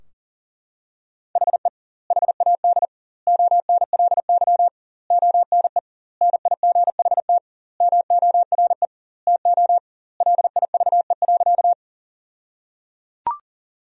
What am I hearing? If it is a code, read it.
Morse code, heard as a keyed tone, about 37 wpm, 700 Hz. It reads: HE HAD ONLY ONE NIGHT MORE TO LIVE.
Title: 2x Word-Spacing - Sentences from Top 300 Words - 40wpm